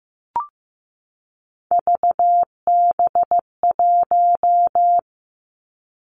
Morse code, heard as a keyed tone, about 15 wpm, 700 Hz. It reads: VB1
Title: Three Challenging Letter-Number Combinations 15wpm